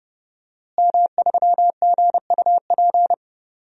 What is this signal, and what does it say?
Morse code, heard as a keyed tone, about 30 wpm, 700 Hz. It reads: M3GUP
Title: DXCC-100 Call Signs - 5 Characters - Repeated 2x - 30wpm